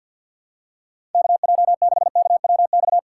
Morse code, heard as a keyed tone, about 52 wpm, 700 Hz. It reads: KJ6KWX
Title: USA Call Signs - 6 Characters - 6x Character Spacing - 50wpm